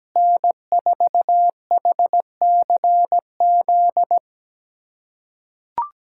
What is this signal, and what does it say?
Morse code, heard as a keyed tone, about 17 wpm, 700 Hz. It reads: N4HCZ
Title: USA Callsign DE Callsign 17wpm